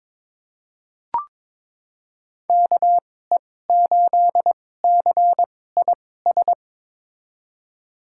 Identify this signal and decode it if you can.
Morse code, heard as a keyed tone, about 22 wpm, spaced as for 16 wpm, 700 Hz. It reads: KE8CIS
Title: USA Call Signs - 6 Characters - 2x Character Spacing - 22wpm